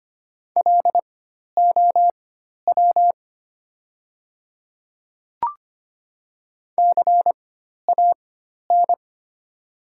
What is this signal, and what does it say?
Morse code, heard as a keyed tone, about 25 wpm, spaced as for 12 wpm, 700 Hz. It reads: LOW CAN
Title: 4x Character Spacing - Top 300 Words - 25wpm